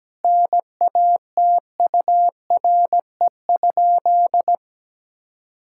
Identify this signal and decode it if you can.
Morse code, heard as a keyed tone, about 17 wpm, 700 Hz. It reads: NATURE?
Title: Sentences from Top 1000 Words - Review Part 5 - 17wpm